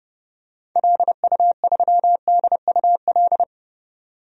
Morse code, heard as a keyed tone, about 30 wpm, 700 Hz. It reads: LU3DUL
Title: DXCC-100 Call Signs - 6 Characters - 30wpm